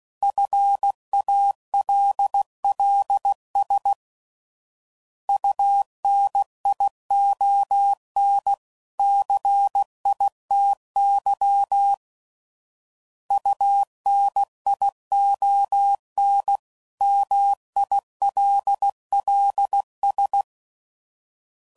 Morse code, heard as a keyed tone, about 16 wpm, 800 Hz. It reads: FALLS UNION CITY UNION MILLS
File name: CW Code Groups @15WPM / fromSM's / Cities